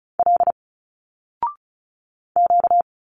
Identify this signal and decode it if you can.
Morse code, heard as a keyed tone, about 36 wpm, 700 Hz. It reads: L Q
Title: Single Letters 35wpm